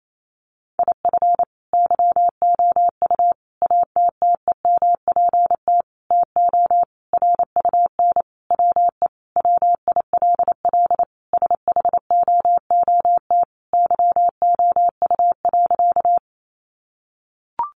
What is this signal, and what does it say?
Morse code, heard as a keyed tone, about 28 wpm, 700 Hz. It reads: IF YOU ATTEMPT TO RUN WE WILL SHOOT YOU.
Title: Sentences from Top 1200 Words 28wpm